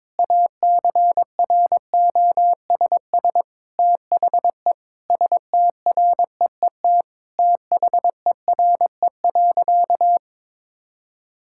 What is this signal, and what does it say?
Morse code, heard as a keyed tone, about 22 wpm, 700 Hz. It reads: ACROSS THE STREET THERE.
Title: Sentences from Top 1000 Words - Review Part 5 - 22wpm